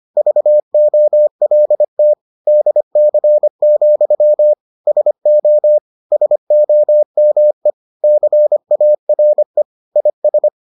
Morse code, heard as a keyed tone, about 25 wpm, 600 Hz. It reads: VOLT DC, SO SOME CARE IS